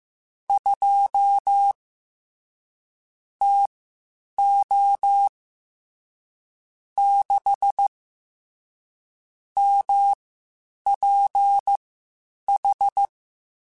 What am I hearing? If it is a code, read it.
Morse code, heard as a keyed tone, about 15 wpm, spaced as for 8 wpm, 800 Hz. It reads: 2 TO 6 MPH
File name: Spaced Characters / Spaced Characters-mp3s / Weather 4 10 sp @13